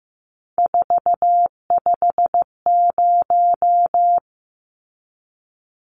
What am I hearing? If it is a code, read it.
Morse code, heard as a keyed tone, about 15 wpm, 700 Hz. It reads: 450
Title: Numbers - 3-Digit 15wpm